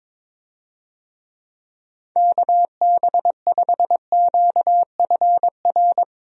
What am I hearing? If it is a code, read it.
Morse code, heard as a keyed tone, about 22 wpm, 700 Hz. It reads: KB5QFR
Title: USA Call Signs - 6 Characters - Repeated 3x - 22wpm